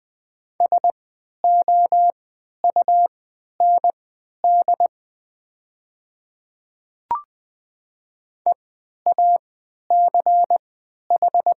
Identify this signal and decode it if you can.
Morse code, heard as a keyed tone, about 20 wpm, spaced as for 11 wpm, 700 Hz. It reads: SOUND EACH
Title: 3x Character Spacing - Top 100 Words - 20wpm